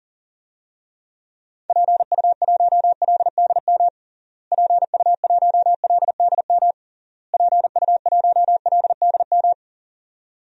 Morse code, heard as a keyed tone, about 40 wpm, 700 Hz. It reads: PU1LDM PU1LDM PU1LDM
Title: DXCC-100 Call Signs - 6 Characters - Repeated 3x - 40wpm